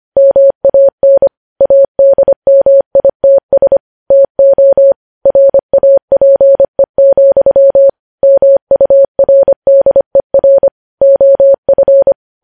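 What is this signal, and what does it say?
Morse code, heard as a keyed tone, about 25 wpm, 550 Hz. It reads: MAN ADMITS TO RAPE, MURDER OF
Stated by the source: Morse code news headlines sent at 25 wpm.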